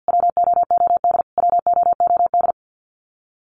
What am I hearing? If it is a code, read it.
Morse code, heard as a keyed tone, about 54 wpm, 700 Hz. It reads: WOOD WOOD
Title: Top 500 Words Repeated 50wpm